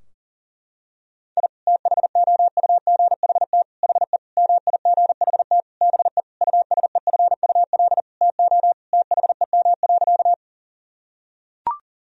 Morse code, heard as a keyed tone, about 37 wpm, 700 Hz. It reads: I THOUGHT HE MIGHT BE USEFUL TO THEM.
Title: Sentences from Top 1000 Words - Review Part 4 - 40wpm